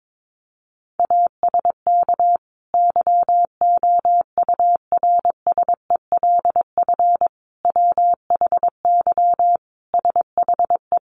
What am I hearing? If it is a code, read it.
Morse code, heard as a keyed tone, about 22 wpm, 700 Hz. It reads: ASK YOURSELF WHY SHE
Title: Sentences from Top 600 Words 22wpm